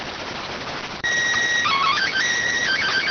SSTV (slow-scan television), PD 290.